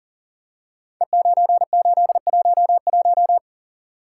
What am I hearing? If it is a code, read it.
Morse code, heard as a keyed tone, about 42 wpm, 690 Hz. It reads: E9811
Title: Code Groups - 5 Letter:Numbers - 40wpm